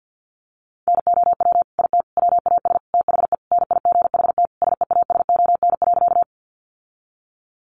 Morse code, heard as a keyed tone, about 50 wpm, 700 Hz. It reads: NOW IT WAS THE NIGHT SEASON.
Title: Sentences from Top 1400 Words 50wpm